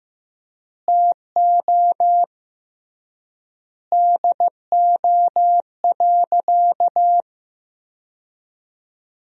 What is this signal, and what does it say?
Morse code, heard as a keyed tone, about 15 wpm, 700 Hz. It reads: TO DO.
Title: 3x Word-Spacing - Sentences from Top 100 Words - 15wpm